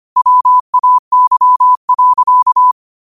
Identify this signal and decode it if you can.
Morse code, heard as a keyed tone, about 25 wpm, 1000 Hz. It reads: WAY.
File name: qotdfiles / 25wpm